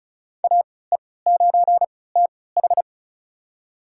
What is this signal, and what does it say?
Morse code, heard as a keyed tone, about 36 wpm, spaced as for 20 wpm, 700 Hz. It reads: AE9TH
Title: USA Call Signs - 5 Characters - 3x Character Spacing - 35wpm